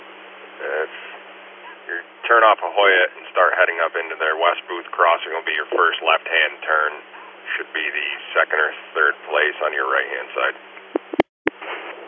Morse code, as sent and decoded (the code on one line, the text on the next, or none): none